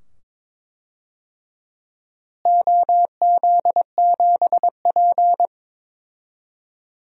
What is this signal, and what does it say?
Morse code, heard as a keyed tone, about 22 wpm, 700 Hz. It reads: OZ7P
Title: DXCC-100 Call Signs - 4 Characters - Repeated 2x - 22wpm